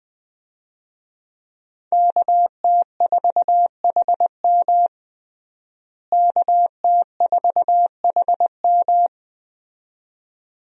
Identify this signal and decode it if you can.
Morse code, heard as a keyed tone, about 20 wpm, 700 Hz. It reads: KT4HM KT4HM
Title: USA Call Signs - 5 Characters - Repeated 2x - 20wpm